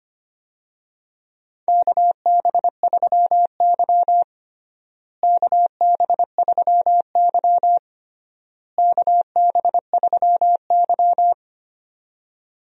Morse code, heard as a keyed tone, about 25 wpm, 700 Hz. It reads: KB3Y KB3Y KB3Y
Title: USA Call Signs - 4 Characters - Repeated 3x - 25wpm